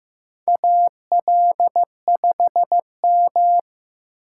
Morse code, heard as a keyed tone, about 15 wpm, 700 Hz. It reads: AL5M
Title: USA Call Signs - 4 Characters - 8x Character Spacing - 15wpm